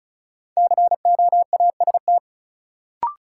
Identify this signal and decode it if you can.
Morse code, heard as a keyed tone, about 36 wpm, 700 Hz. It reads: COAST
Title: Sentences from Top 1000 Words - Review Part 4 - New Word Drill - 35wpm